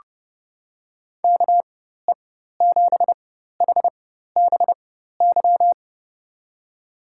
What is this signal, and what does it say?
Morse code, heard as a keyed tone, about 32 wpm, spaced as for 14 wpm, 700 Hz. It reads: KE7HBY
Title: USA Call Signs - 6 Characters - 4x Character Spacing - 30wpm